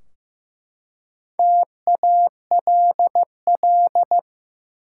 Morse code, heard as a keyed tone, about 15 wpm, 700 Hz. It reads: TALL
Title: Four Letter Words - 15wpm